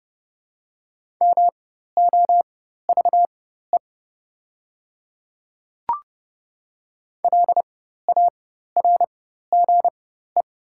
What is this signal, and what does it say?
Morse code, heard as a keyed tone, about 31 wpm, spaced as for 14 wpm, 700 Hz. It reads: MOVE LARGE
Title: 4x Character Spacing - Top 300 Words - 30wpm